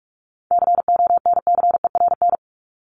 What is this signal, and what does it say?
Morse code, heard as a keyed tone, about 44 wpm, 700 Hz. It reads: CONCERN
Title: Top 751-1000 Words 45wpm